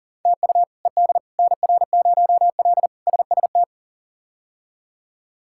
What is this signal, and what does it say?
Morse code, heard as a keyed tone, about 40 wpm, 700 Hz. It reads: TU ED NR0L SST
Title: K1USN SST QSO Practice - 40wpm